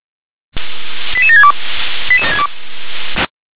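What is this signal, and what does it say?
Five-tone selective call (ZVEI), heard as quick runs of five tones, tones of about 70 ms. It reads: 88762
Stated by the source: ZVEI five-tone selective call received on an SDR.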